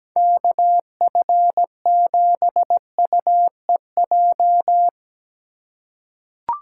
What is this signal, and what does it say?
Morse code, heard as a keyed tone, about 17 wpm, 700 Hz. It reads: KF7UEJ
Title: USA Call Signs - 6 Characters - 4x Character Spacing - 17wpm